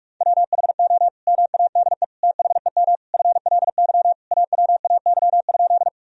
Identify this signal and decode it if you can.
Morse code, heard as a keyed tone, about 46 wpm, 700 Hz. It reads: WHO MADE THEM FLY AWAY?
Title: Sentences from Top 1000 Words - Review Part 2 - 45wpm